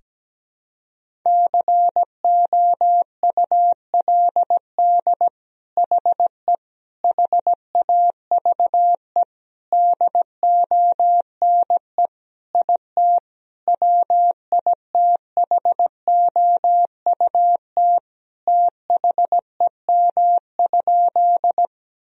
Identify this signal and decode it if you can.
Morse code, heard as a keyed tone, about 17 wpm, 700 Hz. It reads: COULD HE HAVE DONE IT WITHOUT THEM?
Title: Sentences from Top 1000 Words - Review Part 3 - 17wpm